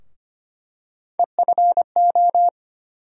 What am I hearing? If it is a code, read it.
Morse code, heard as a keyed tone, about 25 wpm, 700 Hz. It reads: EFO
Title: USA Call Sign Postfixes - 3 Letters 25wpm